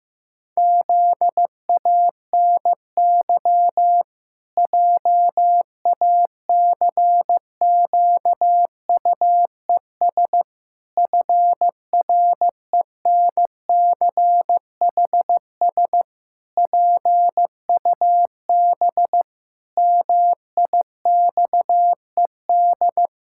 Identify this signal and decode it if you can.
Morse code, heard as a keyed tone, about 15 wpm, 700 Hz. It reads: ZANY JACQUES FRENCHS PUB MIXED
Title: Pangram Challenge v2 - 15wpm